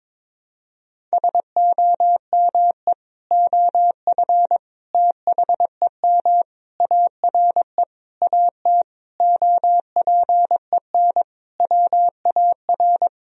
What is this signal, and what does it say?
Morse code, heard as a keyed tone, about 22 wpm, 700 Hz. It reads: SOME OF THEM ARE AT OPEN WAR
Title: Sentences from Top 1000 Words - Review Part 2 - 22wpm